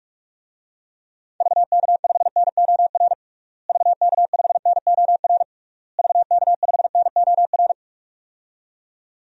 Morse code, heard as a keyed tone, about 46 wpm, 700 Hz. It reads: VK5NOR VK5NOR VK5NOR
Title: DXCC-100 Call Signs - 6 Characters - Repeated 3x - 45wpm